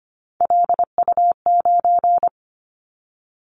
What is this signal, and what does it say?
Morse code, heard as a keyed tone, about 25 wpm, 700 Hz. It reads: LU9
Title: Callsign Prefixes for Top 100 DXCC - 25wpm